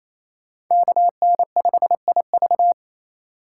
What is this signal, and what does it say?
Morse code, heard as a keyed tone, about 27 wpm, 700 Hz. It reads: KN5IV
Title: Code Groups - 5 Letter:Numbers - 28wpm